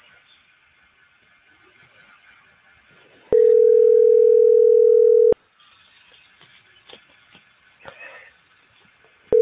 Ringback tone (North America), 440 + 480 Hz.